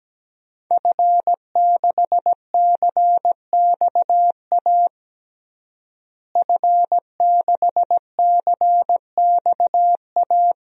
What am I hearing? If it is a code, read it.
Morse code, heard as a keyed tone, about 17 wpm, 700 Hz. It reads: F6CXA F6CXA
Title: DXCC-100 Call Signs - 5 Characters - Repeated 3x - 17wpm